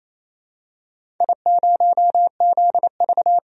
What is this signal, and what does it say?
Morse code, heard as a keyed tone, about 28 wpm, 700 Hz. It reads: I0ZV
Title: DXCC-100 Call Signs - 4 Characters - 28wpm